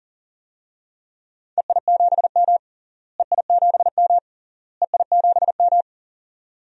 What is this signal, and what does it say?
Morse code, heard as a keyed tone, about 37 wpm, 700 Hz. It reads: EI7M EI7M EI7M
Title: DXCC-100 Call Signs - 4 Characters - Repeated 3x - 40wpm